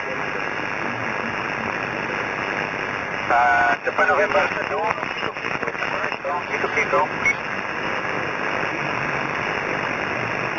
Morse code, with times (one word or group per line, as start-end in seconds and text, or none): none